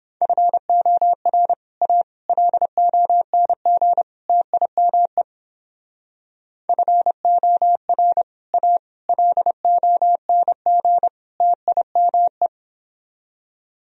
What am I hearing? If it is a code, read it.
Morse code, heard as a keyed tone, about 29 wpm, 700 Hz. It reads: FOR A LONG TIME FOR A LONG TIME